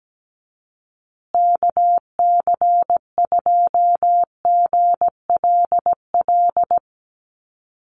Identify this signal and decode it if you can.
Morse code, heard as a keyed tone, about 17 wpm, 700 Hz. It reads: KC2GLL
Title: USA Call Signs Repeated 3x 17wpm